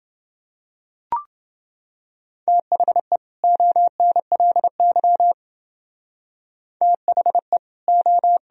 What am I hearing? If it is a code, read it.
Morse code, heard as a keyed tone, about 31 wpm, 700 Hz. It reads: THE ONLY THE O